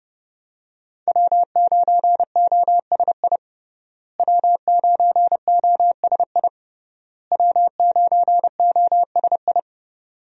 Morse code, heard as a keyed tone, about 30 wpm, 700 Hz. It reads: W9OSI W9OSI W9OSI
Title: USA Call Signs - 5 Characters - Repeated 3x - 30wpm